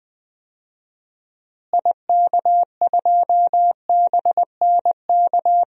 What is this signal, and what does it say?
Morse code, heard as a keyed tone, about 21 wpm, 700 Hz. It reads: IK2BNK
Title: DXCC-100 Call Signs - 6 Characters - Repeated 2x - 20wpm